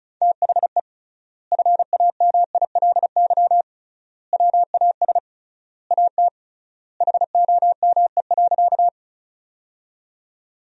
Morse code, heard as a keyed tone, about 36 wpm, 700 Hz. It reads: THE FAMILY WAS AT HOME.
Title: 3x Word-Spacing - Sentences from Top 400 Words - 35wpm